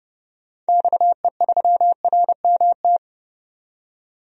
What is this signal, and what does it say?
Morse code, heard as a keyed tone, about 30 wpm, 700 Hz. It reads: XE3RMT
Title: DXCC-100 Call Signs - 6 Characters - 30wpm